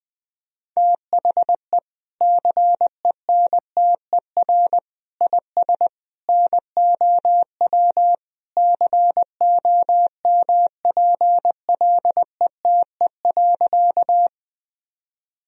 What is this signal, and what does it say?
Morse code, heard as a keyed tone, about 20 wpm, 700 Hz. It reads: THE CENTER IS NOW COMPLETE.
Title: Sentences from Top 1000 Words - Review Part 2 - 20wpm